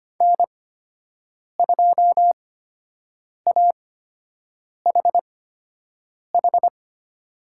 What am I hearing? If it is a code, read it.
Morse code, heard as a keyed tone, about 25 wpm, spaced as for 7 wpm, 700 Hz. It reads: N2AHH
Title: USA Call Signs - 5 Characters - 8x Character Spacing - 25wpm